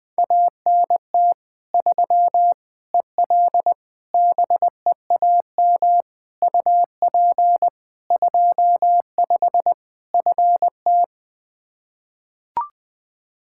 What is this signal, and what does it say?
Morse code, heard as a keyed tone, about 20 wpm, 700 Hz. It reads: ANT 3 EL BEAM UP 25 FT
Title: K1USN SST QSO Practice 20wpm